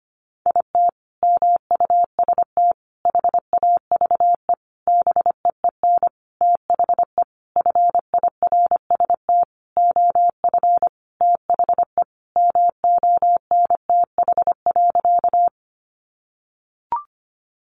Morse code, heard as a keyed tone, about 25 wpm, 700 Hz. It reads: IT MUST HAVE BEEN THE FIRST OF THE MONTH.
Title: Sentences from Top 800 Words 25wpm